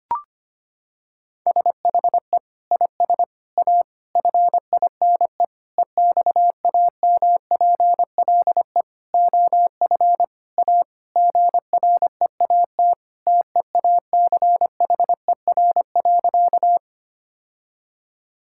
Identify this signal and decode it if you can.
Morse code, heard as a keyed tone, about 25 wpm, 700 Hz. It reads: SHE IS A FINE EXAMPLE OF A GREAT TEACHER.
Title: Sentences from Top 1000 Words - Review Part 6 - 25wpm